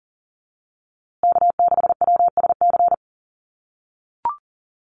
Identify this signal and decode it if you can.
Morse code, heard as a keyed tone, about 40 wpm, 700 Hz. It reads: K6WSC
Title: USA Call Signs Repeated 2x 40wpm